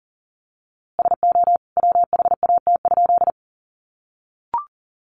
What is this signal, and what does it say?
Morse code, heard as a keyed tone, about 40 wpm, 700 Hz. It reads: SO WHAT?
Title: A Sound Of Thunder 40wpm